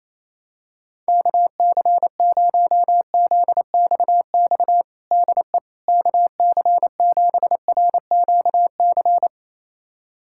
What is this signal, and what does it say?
Morse code, heard as a keyed tone, about 28 wpm, 700 Hz. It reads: KC0ZXX DE KC7RQC
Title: USA Callsign DE Callsign 28wpm